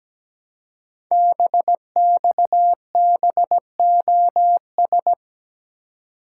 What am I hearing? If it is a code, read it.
Morse code, heard as a keyed tone, about 17 wpm, 700 Hz. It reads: BXBOS
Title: Code Groups - 5 Letters - 17wpm